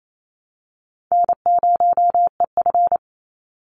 Morse code, heard as a keyed tone, about 28 wpm, 700 Hz. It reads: N0EF
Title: Easy Call Signs Repeated 2x - 28wpm